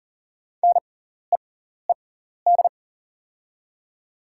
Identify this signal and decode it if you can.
Morse code, heard as a keyed tone, about 38 wpm, spaced as for 14 wpm, 690 Hz. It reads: NEED
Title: 6x Character Spacing - Top 200 Words - 40wpm